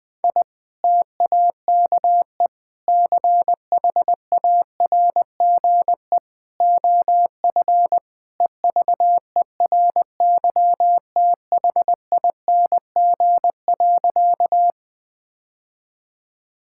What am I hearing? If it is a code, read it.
Morse code, heard as a keyed tone, about 20 wpm, 700 Hz. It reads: I TAKE CHARGE OF EVERYTHING.
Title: Sentences from Top 1000 Words - Review Part 6 - 20wpm